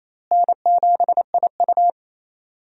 Morse code, heard as a keyed tone, about 28 wpm, 700 Hz. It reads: N7IU
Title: USA Call Signs - 4 Characters - Repeated 3x - 28wpm